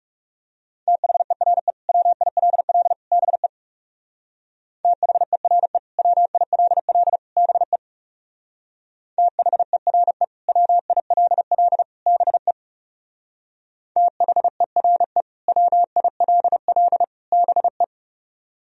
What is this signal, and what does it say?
Morse code, heard as a keyed tone, about 40 wpm, 700 Hz. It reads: THERE WILL BE THERE WILL BE THERE WILL BE THERE WILL BE